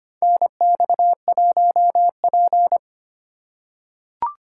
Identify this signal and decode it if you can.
Morse code, heard as a keyed tone, about 25 wpm, 700 Hz. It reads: NX1P
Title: USA Call Signs - 4 Characters - 4x Character Spacing - 25wpm